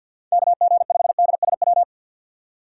Morse code, heard as a keyed tone, about 52 wpm, 700 Hz. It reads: KG5DSW
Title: USA Call Signs - 6 Characters - 3x Character Spacing - 50wpm